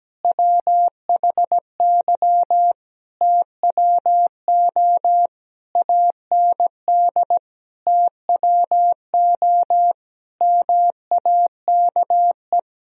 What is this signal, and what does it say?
Morse code, heard as a keyed tone, about 17 wpm, 700 Hz. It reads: WHY TWO AND TWO MAKE
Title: Sentences from Top 1000 Words - Review Part 1 - 17wpm